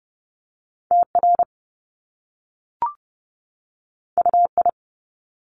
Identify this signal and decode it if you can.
Morse code, heard as a keyed tone, about 30 wpm, 700 Hz. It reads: TR UI
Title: Top Two Letter Combinations in Words 30wpm